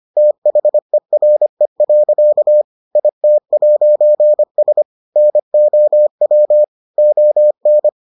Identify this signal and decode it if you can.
Morse code, heard as a keyed tone, about 25 wpm, 600 Hz. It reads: THERE. IT'S NOW ON